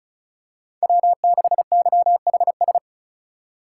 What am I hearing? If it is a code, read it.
Morse code, heard as a keyed tone, about 36 wpm, 700 Hz. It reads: W6YHS
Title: USA Call Signs - 5 Characters - 4x Character Spacing - 35wpm